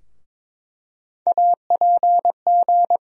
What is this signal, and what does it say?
Morse code, heard as a keyed tone, about 22 wpm, 700 Hz. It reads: APG
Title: USA Call Sign Postfixes - 3 Letters 22wpm